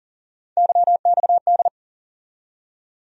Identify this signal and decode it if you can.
Morse code, heard as a keyed tone, about 40 wpm, 700 Hz. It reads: YXD
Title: USA Call Sign Postfixes - 3 Letters 40wpm